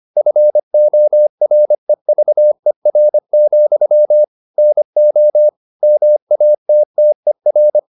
Morse code, heard as a keyed tone, about 25 wpm, 600 Hz. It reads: FOREVER, NO MATTER
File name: the old man and the sea part 1 25wpm